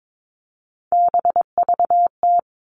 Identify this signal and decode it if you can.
Morse code, heard as a keyed tone, about 22 wpm, 700 Hz. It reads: BVT
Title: All Cut Numbers - 3-Digit 22wpm